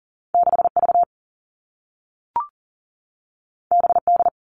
Morse code, heard as a keyed tone, about 40 wpm, 700 Hz. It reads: 6V BD
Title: All Cut Numbers - 2-Digit 40wpm